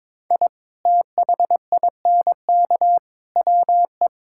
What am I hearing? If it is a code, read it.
Morse code, heard as a keyed tone, about 22 wpm, 700 Hz. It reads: I THINK WE
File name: practice / 22wpm / Build A Phrase - 5-Word Phrases - 22wpm